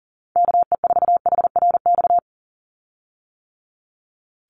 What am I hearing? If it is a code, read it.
Morse code, heard as a keyed tone, about 38 wpm, 700 Hz. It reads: KE4HRX
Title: USA Call Signs 40wpm